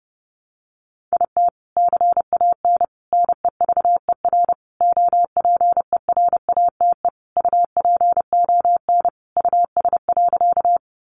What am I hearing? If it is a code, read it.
Morse code, heard as a keyed tone, about 30 wpm, 700 Hz. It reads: IT CAN NEVER OPERATE UPON US.
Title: Sentences from Top 1500 Words 30wpm